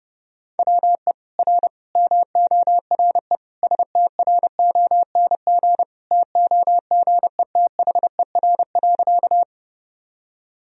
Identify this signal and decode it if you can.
Morse code, heard as a keyed tone, about 30 wpm, 700 Hz. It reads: WE R MORE STRONG TOGETHER.
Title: Sentences from Top 500 Words and Abbreviations 30wpm